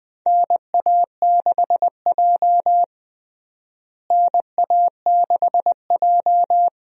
Morse code, heard as a keyed tone, about 20 wpm, 700 Hz. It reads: NA6J NA6J
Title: USA Call Signs - 4 Characters - Repeated 3x - 20wpm